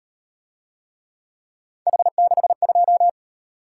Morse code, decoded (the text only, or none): H62